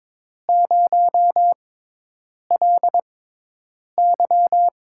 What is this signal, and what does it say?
Morse code, heard as a keyed tone, about 22 wpm, spaced as for 8 wpm, 700 Hz. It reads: 0LY
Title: USA Call Signs - 6 Characters - 6x Character Spacing - 22wpm